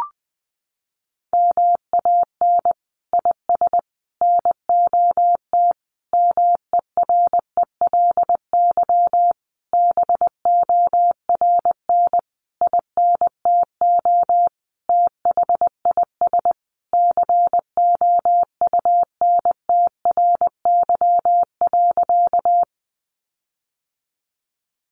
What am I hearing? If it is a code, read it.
Morse code, heard as a keyed tone, about 20 wpm, 700 Hz. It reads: MAN IS NOT MERELY BORN INTO THIS COUNTRY.
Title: Sentences from Top 1400 Words 20wpm